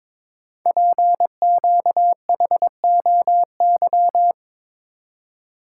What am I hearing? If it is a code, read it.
Morse code, heard as a keyed tone, about 22 wpm, 700 Hz. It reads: PQHOY
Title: Code Groups - 5 Letters - 22wpm